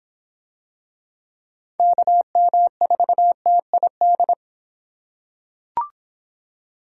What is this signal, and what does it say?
Morse code, heard as a keyed tone, about 26 wpm, 700 Hz. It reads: KM4TID